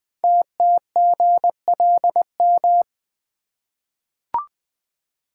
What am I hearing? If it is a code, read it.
Morse code, heard as a keyed tone, about 20 wpm, 700 Hz. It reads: TTGLM
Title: Code Groups - 5 Letters - 20wpm